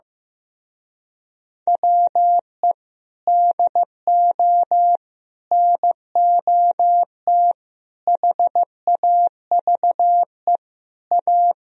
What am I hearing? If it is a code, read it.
Morse code, heard as a keyed tone, about 15 wpm, 700 Hz. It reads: WE DO NOT HAVE A
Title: Speed-Racing - Sets of 5 Words 15wpm